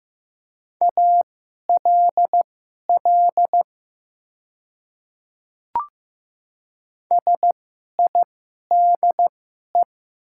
Morse code, heard as a keyed tone, about 15 wpm, spaced as for 11 wpm, 700 Hz. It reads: ALL SIDE
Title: 2x Character Spacing - Top 100 Words - 15wpm